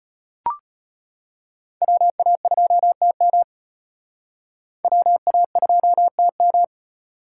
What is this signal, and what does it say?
Morse code, heard as a keyed tone, about 34 wpm, 700 Hz. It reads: WA2TM WA2TM